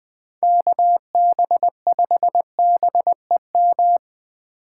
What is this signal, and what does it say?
Morse code, heard as a keyed tone, about 24 wpm, 700 Hz. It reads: KB5BEM
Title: Speed Racing - USA 2x3 Call Signs 20wpm